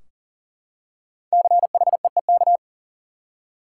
Morse code, heard as a keyed tone, about 40 wpm, 700 Hz. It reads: CHEEK
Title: Five Letter Words - 40wpm